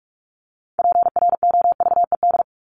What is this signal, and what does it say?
Morse code, heard as a keyed tone, about 44 wpm, 700 Hz. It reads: PROVED
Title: ED Suffix - Top 100 Words with Base Words 45wpm